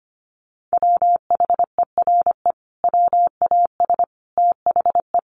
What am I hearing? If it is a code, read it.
Morse code, heard as a keyed tone, about 25 wpm, 700 Hz. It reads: WHERE WAS THE
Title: Sentences from Top 1400 Words 25wpm